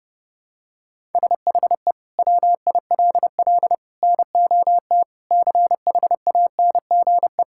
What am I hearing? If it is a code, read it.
Morse code, heard as a keyed tone, about 30 wpm, 700 Hz. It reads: SHE WILL NOT CHANGE
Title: Sentences from Top 1000 Words - Review Part 6 - 30wpm